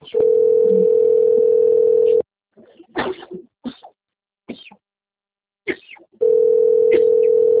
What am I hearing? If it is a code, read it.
Ringback tone (North America), 440 + 480 Hz.
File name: rg-1600-unknown- - - .87193